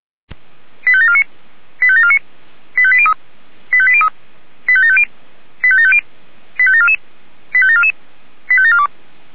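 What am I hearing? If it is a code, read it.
Five-tone selective call (ZVEI), heard as quick runs of five tones, tones of about 70 ms. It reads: 85749 85749 85803 85803 85750 85750 85744 85744 85742
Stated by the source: ZVEI five-tone selective call received on SDR.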